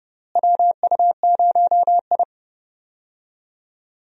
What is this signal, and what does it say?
Morse code, heard as a keyed tone, about 30 wpm, 700 Hz. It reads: WU0I
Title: USA Call Signs - 4 Characters - Repeated 3x - 30wpm